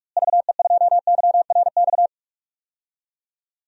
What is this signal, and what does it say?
Morse code, heard as a keyed tone, about 46 wpm, 700 Hz. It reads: VE2YRX
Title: DXCC-100 Call Signs - 6 Characters - Repeated 3x - 45wpm